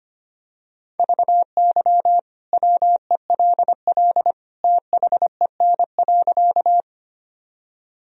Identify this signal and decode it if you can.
Morse code, heard as a keyed tone, about 25 wpm, 700 Hz. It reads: VY WELL THEN.
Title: Sentences from Top 200 Words and Abbreviations 25wpm